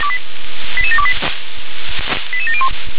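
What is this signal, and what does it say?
Five-tone selective call (ZVEI), heard as quick runs of five tones, tones of about 70 ms. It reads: 88728 88701
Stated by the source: ZVEI five-tone call received by SDR.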